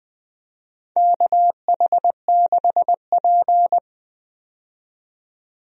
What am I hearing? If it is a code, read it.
Morse code, heard as a keyed tone, about 20 wpm, 700 Hz. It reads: KH6P
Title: DXCC-100 Call Signs - 4 Characters - 20wpm